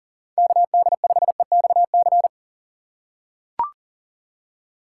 Morse code, heard as a keyed tone, about 38 wpm, 700 Hz. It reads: KD5EXC
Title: USA Call Signs - 6 Characters - 4x Character Spacing - 40wpm